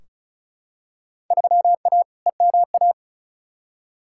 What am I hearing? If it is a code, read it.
Morse code, heard as a keyed tone, about 34 wpm, 700 Hz. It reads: 3A EMA
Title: Field Day Exchanges - 35wpm